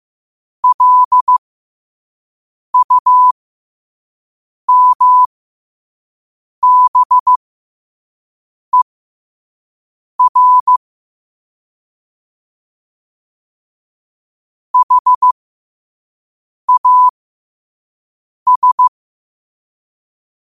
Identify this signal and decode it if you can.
Morse code, heard as a keyed tone, about 15 wpm, spaced as for 5 wpm, 1000 Hz. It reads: LUMBER HAS
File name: qotdfiles / 5wpm